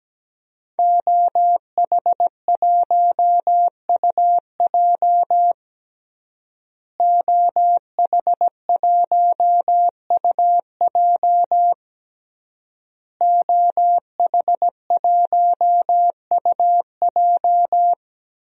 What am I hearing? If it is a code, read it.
Morse code, heard as a keyed tone, about 17 wpm, 700 Hz. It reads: OH1UJ OH1UJ OH1UJ
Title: DXCC-100 Call Signs - 5 Characters - Repeated 3x - 17wpm